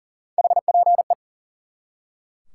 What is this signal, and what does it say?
Morse code, heard as a keyed tone, about 40 wpm, 700 Hz. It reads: HPE
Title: Top 90 QSO Elements 40wpm